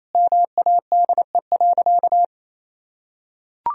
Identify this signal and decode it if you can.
Morse code, heard as a keyed tone, about 27 wpm, 700 Hz. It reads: MADE.
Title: Sentences from Top 1000 Words - Review Part 2 - 28wpm